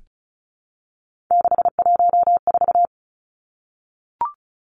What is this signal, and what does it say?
Morse code, heard as a keyed tone, about 34 wpm, 700 Hz. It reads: 614
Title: Numbers - 3-Digit 35wpm